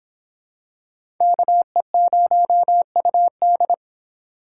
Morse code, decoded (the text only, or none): KE0UD